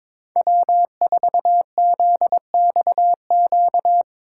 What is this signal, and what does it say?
Morse code, heard as a keyed tone, about 22 wpm, 700 Hz. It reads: W4ZXQ
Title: USA Call Signs - 5 Characters - 4x Character Spacing - 22wpm